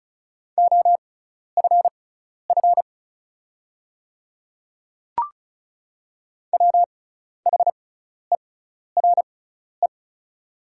Morse code, heard as a keyed tone, about 36 wpm, spaced as for 12 wpm, 690 Hz. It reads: OFF WHERE
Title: 6x Character Spacing - Top 200 Words - 35wpm